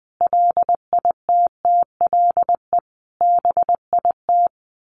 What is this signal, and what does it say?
Morse code, heard as a keyed tone, about 19 wpm, 700 Hz. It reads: LITTLE BIT
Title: Sets of 5 Words Encore2 20wpm